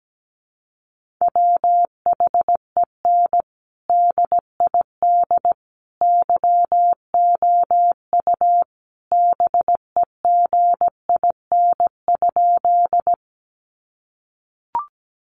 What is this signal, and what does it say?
Morse code, heard as a keyed tone, about 17 wpm, 700 Hz. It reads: WHEN DID YOU BEGIN?
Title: Sentences from Top 300 Words 17wpm